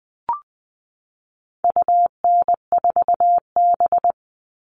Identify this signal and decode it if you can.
Morse code, heard as a keyed tone, about 20 wpm, 700 Hz. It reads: UN4B
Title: All Cut Numbers - 4-Digit 20wpm